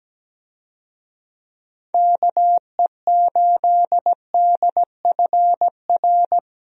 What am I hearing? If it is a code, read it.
Morse code, heard as a keyed tone, about 17 wpm, 700 Hz. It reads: KE8DFR
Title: USA Call Signs - 6 Characters - Repeated 3x - 17wpm